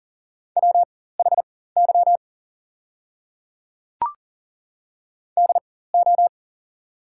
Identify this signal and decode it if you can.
Morse code, heard as a keyed tone, about 41 wpm, spaced as for 19 wpm, 690 Hz. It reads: WHY DO
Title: 4x Character Spacing - Top 300 Words - 40wpm